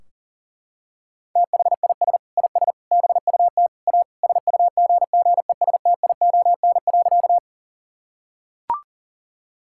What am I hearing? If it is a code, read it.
Morse code, heard as a keyed tone, about 37 wpm, 700 Hz. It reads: THIS IS BUT A SUGGESTION.
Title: Sentences from Top 1600 Words 40wpm